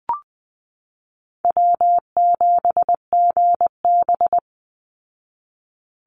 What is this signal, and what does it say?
Morse code, heard as a keyed tone, about 20 wpm, 700 Hz. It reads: W7GB
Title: Easy Call Signs Repeated 3x - 20wpm